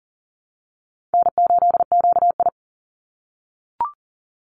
Morse code, heard as a keyed tone, about 40 wpm, 700 Hz. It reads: N8QI
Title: USA Call Signs 40wpm